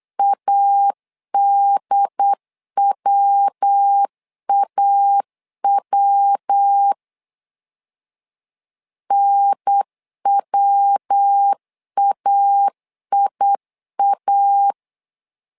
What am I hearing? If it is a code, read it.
Morse code, heard as a keyed tone, about 8 wpm, 800 Hz. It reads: ADWAW NWAIA
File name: LEZIONI / 3 MWRO / 3 MW 8wpm 2501